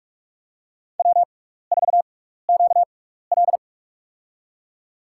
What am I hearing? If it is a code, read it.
Morse code, heard as a keyed tone, about 47 wpm, spaced as for 16 wpm, 700 Hz. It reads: W4QL
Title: USA Call Signs - 4 Characters - 6x Character Spacing - 45wpm